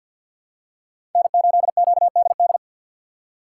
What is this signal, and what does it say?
Morse code, heard as a keyed tone, about 51 wpm, 690 Hz. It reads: N8QDD
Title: USA Call Signs - 5 Characters - 3x Character Spacing - 50wpm